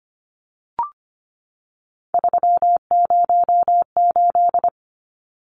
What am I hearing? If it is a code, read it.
Morse code, heard as a keyed tone, about 25 wpm, 700 Hz. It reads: 308
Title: Numbers - 3-Digit 25wpm